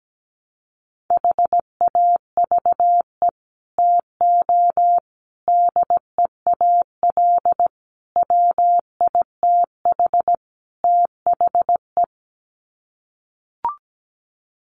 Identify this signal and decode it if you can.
Morse code, heard as a keyed tone, about 17 wpm, 700 Hz. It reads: HAVE TO DEAL WITH THE
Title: Sets of 5 Words Encore2 17wpm